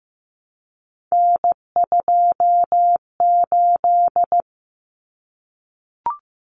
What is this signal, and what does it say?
Morse code, heard as a keyed tone, about 15 wpm, 700 Hz. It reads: N28
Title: Common Cut Numbers - 3-Digit 15wpm